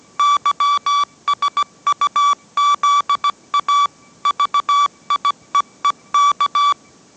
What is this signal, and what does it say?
Morse code, heard as a keyed tone, about 17 wpm, 1150 Hz. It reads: YSUZA VIEEK